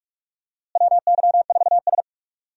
Morse code, heard as a keyed tone, about 46 wpm, 700 Hz. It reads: WY4S
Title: USA Call Signs - 4 Characters - Repeated 2x - 45wpm